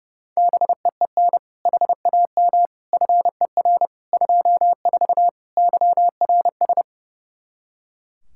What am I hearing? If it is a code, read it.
Morse code, heard as a keyed tone, about 30 wpm, 700 Hz. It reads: BEEN HAM FER 24 YRS
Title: Practice QSO - Exchange 3 - More Info - 30wpm